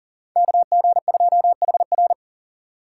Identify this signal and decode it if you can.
Morse code, heard as a keyed tone, about 52 wpm, 700 Hz. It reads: KG2HR